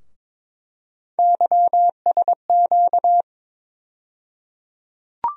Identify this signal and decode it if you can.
Morse code, heard as a keyed tone, about 22 wpm, 700 Hz. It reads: YSQ
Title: USA Call Sign Postfixes - 3 Letters 22wpm